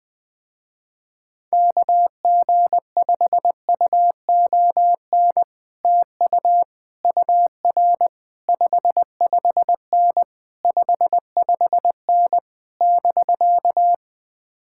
Morse code, heard as a keyed tone, about 20 wpm, 700 Hz. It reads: KG5UON TU UR 55N 55N <BK>
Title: Speed-Racing Common QSO Phrases - 17wpm